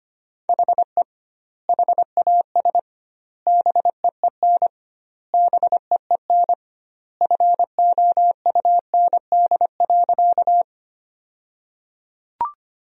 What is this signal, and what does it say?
Morse code, heard as a keyed tone, about 25 wpm, 700 Hz. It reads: HE HAS BEEN BEEN FOUND.
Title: 2x Word-Spacing - Sentences from Top 300 Words - 25wpm